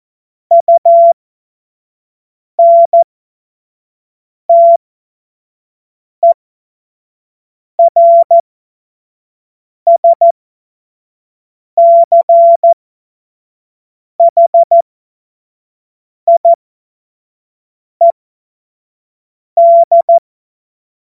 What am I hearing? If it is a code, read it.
Morse code, heard as a keyed tone, about 14 wpm, spaced as for 5 wpm, 680 Hz. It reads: UNTERSCHIED
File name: Morsekurs / Witze / Witz 29 April 2025 25 F70